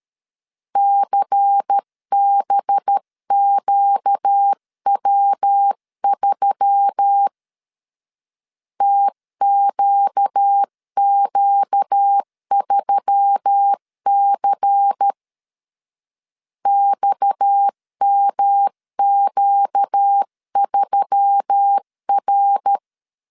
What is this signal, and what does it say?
Morse code, heard as a keyed tone, about 13 wpm, 800 Hz. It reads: CBQW3 TQQ3C XMQ3R
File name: CW / DET / 1 6 1500C S C 12wpm 3533